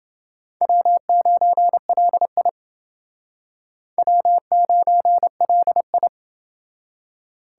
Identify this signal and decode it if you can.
Morse code, heard as a keyed tone, about 28 wpm, 700 Hz. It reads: W9LI W9LI